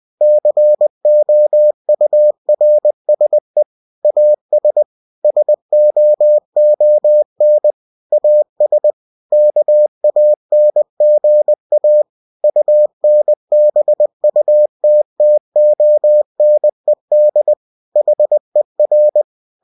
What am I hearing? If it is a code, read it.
Morse code, heard as a keyed tone, about 20 wpm, 600 Hz. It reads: COURSE AS SOON AS KANGA UNBUTTONED HER
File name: cw-books / winnie-the-pooh / winnie-20wpm-page-0071